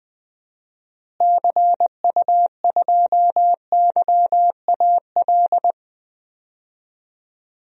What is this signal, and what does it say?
Morse code, heard as a keyed tone, about 20 wpm, 700 Hz. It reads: CU2YAL
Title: DXCC-100 Call Signs - 6 Characters - Repeated 2x - 20wpm